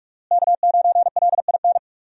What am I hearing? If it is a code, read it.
Morse code, heard as a keyed tone, about 46 wpm, 700 Hz. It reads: K9LIN
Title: USA Call Signs - 5 Characters - Repeated 3x - 45wpm